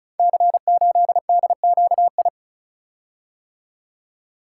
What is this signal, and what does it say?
Morse code, heard as a keyed tone, about 36 wpm, 700 Hz. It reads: C8DQI
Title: Code Groups - 5 Letter:Numbers - 35wpm